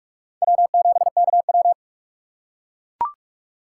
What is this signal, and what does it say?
Morse code, heard as a keyed tone, about 38 wpm, 700 Hz. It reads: W7KW